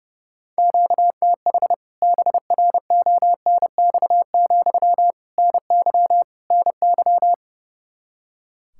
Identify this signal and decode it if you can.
Morse code, heard as a keyed tone, about 32 wpm, 700 Hz. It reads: QTH BRONX, NY NY
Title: K1USN SST QSO Practice 30wpm